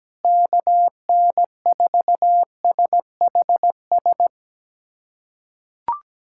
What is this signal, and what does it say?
Morse code, heard as a keyed tone, about 17 wpm, 700 Hz. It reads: KN4SHS
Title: USA Callsign DE Callsign 17wpm